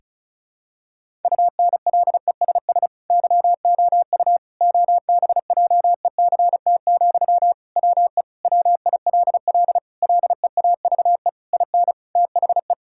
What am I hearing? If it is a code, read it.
Morse code, heard as a keyed tone, about 36 wpm, 700 Hz. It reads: UNLESS YOU OBJECT, WE WILL LEAVE IN THE
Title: Sentences from Top 1000 Words - Review Part 6 - 35wpm